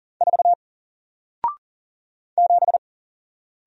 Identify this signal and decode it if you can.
Morse code, heard as a keyed tone, about 40 wpm, 700 Hz. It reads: 4 7
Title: Single Letter-Number 40wpm